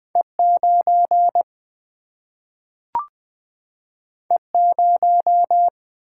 Morse code, heard as a keyed tone, about 20 wpm, 700 Hz. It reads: E 9 E 0
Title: Single-Letter Single-Number 20wpm